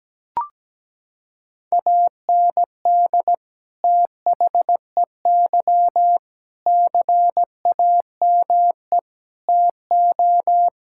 Morse code, heard as a keyed tone, about 17 wpm, 700 Hz. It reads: AND THEY CAME TO
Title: Sentences from Top 1600 Words 17wpm